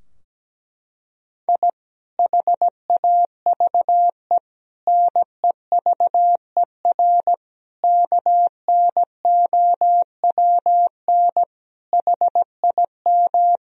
Morse code, heard as a keyed tone, about 17 wpm, 700 Hz. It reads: I HAVE NEVER KNOWN HIM
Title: Sentences from Top 1000 Words - Review Part 3 - 17wpm